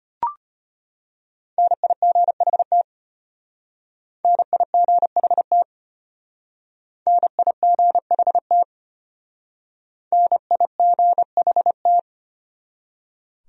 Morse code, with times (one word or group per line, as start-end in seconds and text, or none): 1.6-2.8 NIGHT
4.2-5.6 NIGHT
7.1-8.6 NIGHT
10.1-12.0 NIGHT